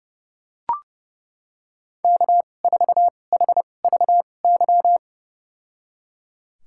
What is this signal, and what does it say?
Morse code, heard as a keyed tone, about 32 wpm, spaced as for 22 wpm, 700 Hz. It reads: K4HVY
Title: USA Call Signs - 5 Characters - 2x Character Spacing - 30wpm